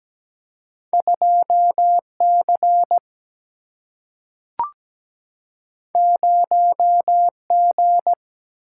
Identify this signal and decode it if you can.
Morse code, heard as a keyed tone, about 17 wpm, 700 Hz. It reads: 2C 0G
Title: Single-Letter Single-Number 17wpm